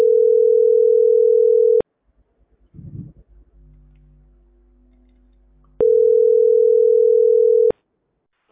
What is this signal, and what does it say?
Ringback tone (North America), 440 + 480 Hz.